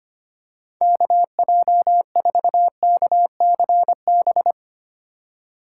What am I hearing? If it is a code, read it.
Morse code, heard as a keyed tone, about 25 wpm, 700 Hz. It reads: KJ4KCB
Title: USA Call Signs - 6 Characters - 4x Character Spacing - 25wpm